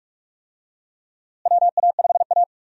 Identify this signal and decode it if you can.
Morse code, heard as a keyed tone, about 41 wpm, 700 Hz. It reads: WA5A
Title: USA Call Signs - 4 Characters - Repeated 2x - 45wpm